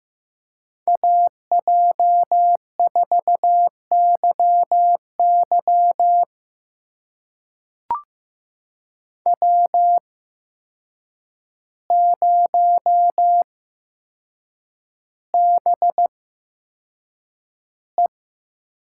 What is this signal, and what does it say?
Morse code, heard as a keyed tone, about 15 wpm, spaced as for 7 wpm, 700 Hz. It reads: AJ4YY W0BE
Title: USA Call Signs - 8x Character Spacing - 15wpm